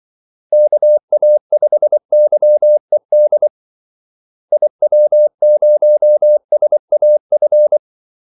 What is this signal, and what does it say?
Morse code, heard as a keyed tone, about 24 wpm, 600 Hz. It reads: KA5YED IW0SAF
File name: Callsigns-24wpm